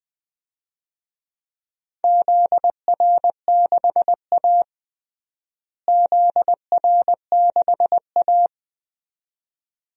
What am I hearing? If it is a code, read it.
Morse code, heard as a keyed tone, about 20 wpm, 700 Hz. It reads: ZR6A ZR6A
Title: DXCC-100 Call Signs - 4 Characters - Repeated 2x - 20wpm